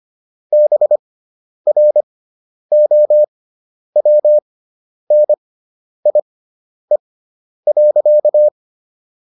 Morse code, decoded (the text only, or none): BROWNIE.